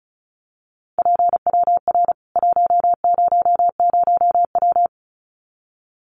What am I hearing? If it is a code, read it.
Morse code, heard as a keyed tone, about 34 wpm, 700 Hz. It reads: PWR 100W
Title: Common QSO Phrases 35wpm